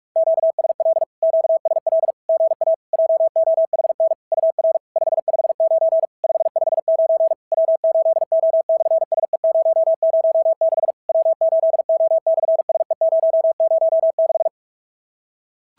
Morse code, heard as a keyed tone, about 46 wpm, 650 Hz. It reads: QSL QSL GA JOHN UR 559 559 W8O/SE006 W8O/SE006
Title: SOTA QSO Practice 45wpm